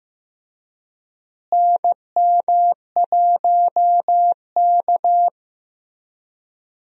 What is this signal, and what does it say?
Morse code, heard as a keyed tone, about 15 wpm, 700 Hz. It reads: NM1K
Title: USA Call Signs - 4 Characters - Repeated 3x - 15wpm